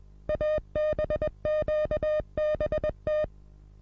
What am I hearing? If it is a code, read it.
Morse code, heard as a keyed tone, about 21 wpm, 610 Hz. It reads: ABQBT
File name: repos / Morsekurs / DJ4UF Morsen26